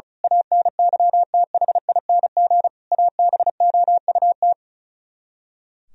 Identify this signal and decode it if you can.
Morse code, heard as a keyed tone, about 35 wpm, 700 Hz. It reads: ANYTHING ABOUT
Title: Speed-Racing - Sets of 5 Words 35wpm